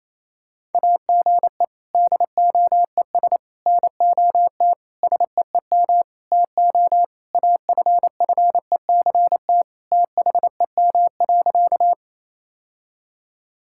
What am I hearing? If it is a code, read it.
Morse code, heard as a keyed tone, about 28 wpm, 700 Hz. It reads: AGE DOES NOT SEEM TO AFFECT THEM.
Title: Sentences from Top 1000 Words - Review Part 6 - 28wpm